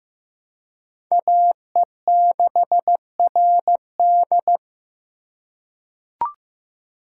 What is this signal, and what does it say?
Morse code, heard as a keyed tone, about 15 wpm, 700 Hz. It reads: AE6RD
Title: USA Call Signs - 5 Characters - 3x Character Spacing - 15wpm